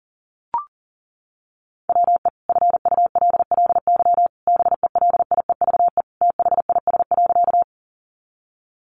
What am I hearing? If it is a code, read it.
Morse code, heard as a keyed tone, about 40 wpm, 700 Hz. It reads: WE FULLY BELIEVE THIS.
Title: Sentences from Top 1300 Words 40wpm